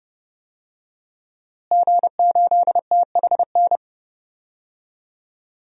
Morse code, heard as a keyed tone, about 30 wpm, 700 Hz. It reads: G8THN
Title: DXCC-100 Call Signs - 5 Characters - Repeated 2x - 30wpm